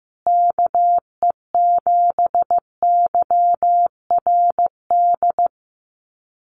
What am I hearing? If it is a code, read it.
Morse code, heard as a keyed tone, about 15 wpm, 700 Hz. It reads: KE7YRD
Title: USA Call Signs Repeated 2x 15wpm